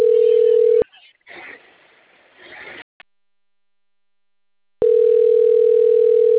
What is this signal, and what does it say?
Ringback tone (North America), 440 + 480 Hz.
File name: rg-1600-unknown- - - .14464